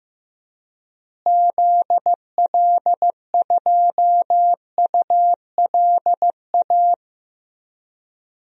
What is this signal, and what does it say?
Morse code, heard as a keyed tone, about 15 wpm, 700 Hz. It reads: ZL2ULA
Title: DXCC-100 Call Signs - 6 Characters - Repeated 2x - 15wpm